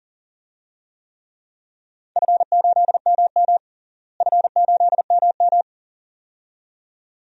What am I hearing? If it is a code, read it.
Morse code, heard as a keyed tone, about 42 wpm, 690 Hz. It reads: F8MM F8MM
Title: DXCC-100 Call Signs - 4 Characters - Repeated 2x - 40wpm